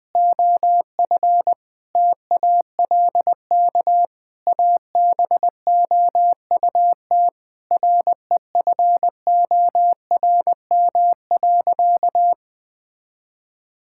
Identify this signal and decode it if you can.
Morse code, heard as a keyed tone, about 20 wpm, 700 Hz. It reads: OF TALK ABOUT REFORM.
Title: Sentences with 7-Word Phrases - 20wpm